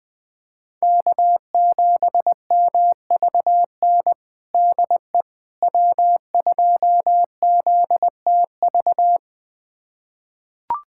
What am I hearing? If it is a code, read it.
Morse code, heard as a keyed tone, about 20 wpm, 700 Hz. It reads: K7MVN DE W2ZTV
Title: USA Callsign DE Callsign 20wpm